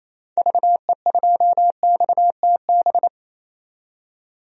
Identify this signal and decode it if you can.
Morse code, heard as a keyed tone, about 28 wpm, 700 Hz. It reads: VE2XTB
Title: DXCC-100 Call Signs - 6 Characters - 28wpm